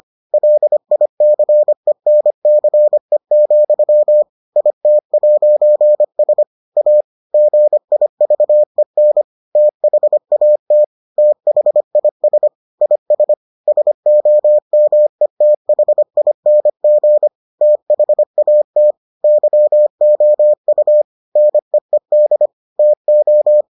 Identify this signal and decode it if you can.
Morse code, heard as a keyed tone, about 25 wpm, 600 Hz. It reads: LICENCE, IT'S A GIVEN THAT THIS IS SOMETHING THAT YOU NEED TO